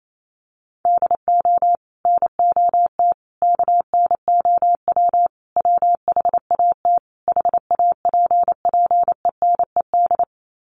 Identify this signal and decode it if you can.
Morse code, heard as a keyed tone, about 28 wpm, 700 Hz. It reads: DO NOT KNOW WHAT HAPPENED
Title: Sets of 5 Words Encore2 28wpm